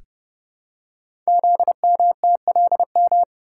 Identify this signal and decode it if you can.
Morse code, heard as a keyed tone, about 30 wpm, 700 Hz. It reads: ZMTLM
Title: Code Groups - 5 Letters - 30wpm